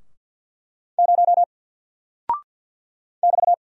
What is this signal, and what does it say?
Morse code, heard as a keyed tone, about 50 wpm, 700 Hz. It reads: 0 <BT>
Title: Single Character (Letter:Number:Punctuation:Prosign) Alt 50wpm